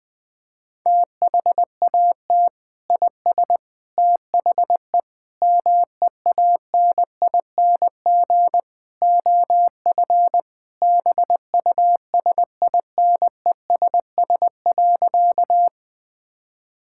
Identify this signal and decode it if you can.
Morse code, heard as a keyed tone, about 19 wpm, 700 Hz. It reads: THAT IS THE MEANING OF BUSINESS.
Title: Sentences from Top 1000 Words - Review Part 6 - 20wpm